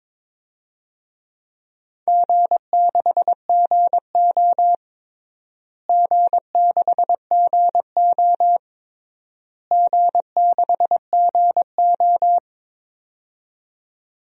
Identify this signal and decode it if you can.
Morse code, heard as a keyed tone, about 22 wpm, 700 Hz. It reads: G6GO G6GO G6GO
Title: DXCC-100 Call Signs - 4 Characters - Repeated 3x - 22wpm